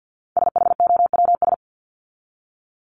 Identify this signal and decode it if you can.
Morse code, heard as a keyed tone, about 50 wpm, 700 Hz. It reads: SHOWS
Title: ES:S Suffix - Top 100 Words with Base Words 50wpm